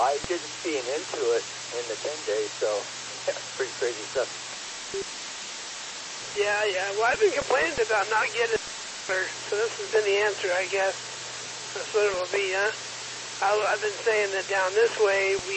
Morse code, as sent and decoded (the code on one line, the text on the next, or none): none